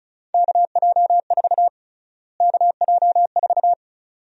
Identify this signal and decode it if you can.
Morse code, heard as a keyed tone, about 34 wpm, 700 Hz. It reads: KJ4 KJ4
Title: USA Call Sign Prefixes - Repeated 2x - 35wpm